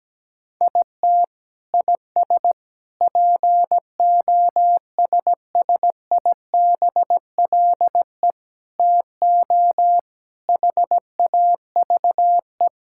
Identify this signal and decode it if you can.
Morse code, heard as a keyed tone, about 17 wpm, 700 Hz. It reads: IT IS POSSIBLE TO HAVE
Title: Sentences from Top 1600 Words 17wpm